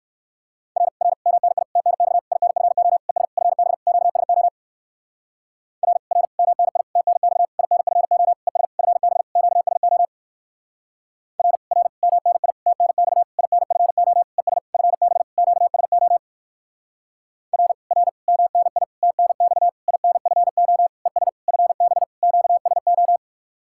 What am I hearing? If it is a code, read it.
Morse code, heard as a keyed tone, about 58 wpm, 700 Hz. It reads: R R MNI TNX INFO ES FB QSO R R MNI TNX INFO ES FB QSO R R MNI TNX INFO ES FB QSO R R MNI TNX INFO ES FB QSO